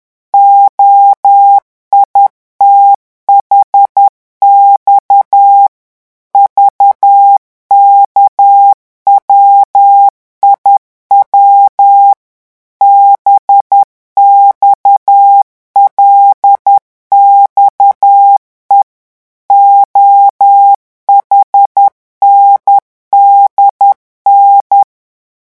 Morse code, heard as a keyed tone, about 10 wpm, 800 Hz. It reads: OITHX VKWIW BXLXE OHNDN
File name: morse mp3 / revision8-vitesse 10 mots